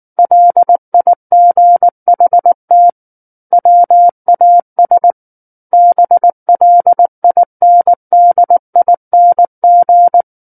Morse code, heard as a keyed tone, about 19 wpm, 700 Hz. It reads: LIGHT WAS BLINDING
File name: Book One / 10-Chapter 10 - 19 WPM